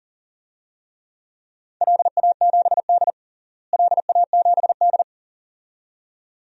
Morse code, heard as a keyed tone, about 40 wpm, 690 Hz. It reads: LA7D LA7D
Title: DXCC-100 Call Signs - 4 Characters - Repeated 2x - 40wpm